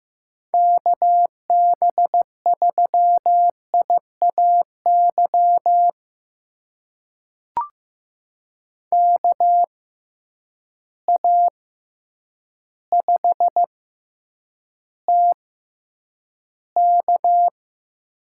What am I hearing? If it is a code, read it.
Morse code, heard as a keyed tone, about 15 wpm, spaced as for 8 wpm, 700 Hz. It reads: KB3IAY KA5TK
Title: USA Call Signs - 6 Characters - 6x Character Spacing - 15wpm